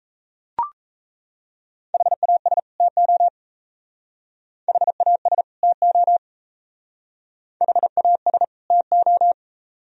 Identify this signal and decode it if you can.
Morse code, heard as a keyed tone, about 34 wpm, 700 Hz. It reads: HAS TO HAS TO HAS TO
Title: Speed-Racing - Sets of 2 Words 28wpm